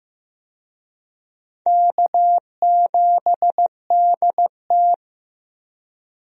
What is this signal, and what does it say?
Morse code, heard as a keyed tone, about 15 wpm, 700 Hz. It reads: K7DT
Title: USA Call Signs - 4 Characters - Repeated 2x - 15wpm